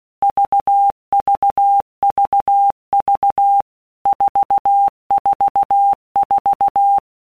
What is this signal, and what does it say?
Morse code, heard as a keyed tone, about 16 wpm, 800 Hz. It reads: VVVV 444
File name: Basic Letter Groups By WPM's / Groups@15 / VB46@15